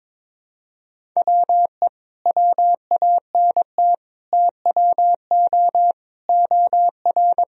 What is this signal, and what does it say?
Morse code, heard as a keyed tone, about 21 wpm, 700 Hz. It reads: WE WANT TWO OR
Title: Sentences from Top 1000 Words - Review Part 1 - 22wpm